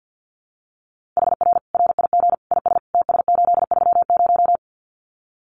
Morse code, heard as a keyed tone, about 50 wpm, 700 Hz. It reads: HR RIG IS TS830
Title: Common QSO Phrases 50wpm